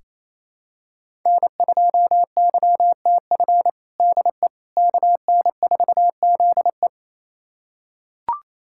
Morse code, heard as a keyed tone, about 28 wpm, 700 Hz. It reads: N2YTF DE KN4ZE
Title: USA Callsign DE Callsign - 5 Characters - 28wpm